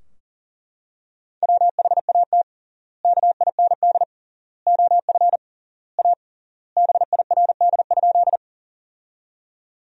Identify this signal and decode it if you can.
Morse code, heard as a keyed tone, about 40 wpm, 700 Hz. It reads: WHAT KIND OF A BIRD?
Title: 3x Word-Spacing - Sentences from Top 400 Words - 40wpm